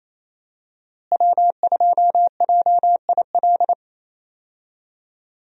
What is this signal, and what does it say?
Morse code, heard as a keyed tone, about 28 wpm, 700 Hz. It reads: W2JIL
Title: USA Call Signs - 5 Characters - Repeated 3x - 28wpm